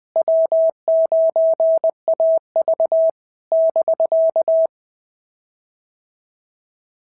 Morse code, heard as a keyed tone, about 20 wpm, 650 Hz. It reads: W9AV <BK>
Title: POTA QSO Practice - 20wpm